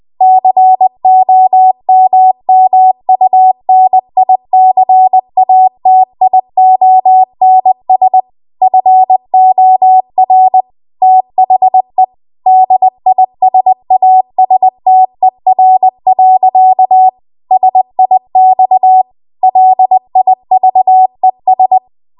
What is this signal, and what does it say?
Morse code, heard as a keyed tone, about 20 wpm, 750 Hz. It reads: COMMUNICATIONS FOR THE DISASTER. SIX LIVES
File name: Archive / 20 WPM / 20WPM